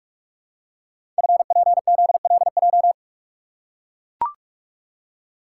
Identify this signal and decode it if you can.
Morse code, heard as a keyed tone, about 41 wpm, 700 Hz. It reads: FPZLJ
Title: Code Groups - 5 Letter:Numbers - 45wpm